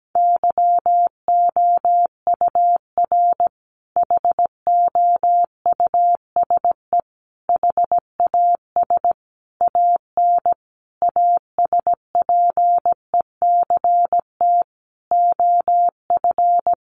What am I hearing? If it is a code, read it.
Morse code, heard as a keyed tone, about 17 wpm, 700 Hz. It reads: YOUR HOUSE HAS AN ASPECT OF
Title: Sentences from Top 1300 Words 17wpm